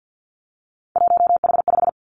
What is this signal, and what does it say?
Morse code, heard as a keyed tone, about 50 wpm, 700 Hz. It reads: 1H5
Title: Three Challenging Letter-Number Combinations 50wpm